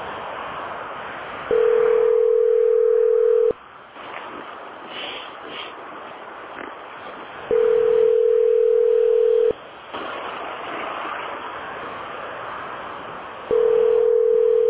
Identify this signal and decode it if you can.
Ringback tone (North America), 440 + 480 Hz.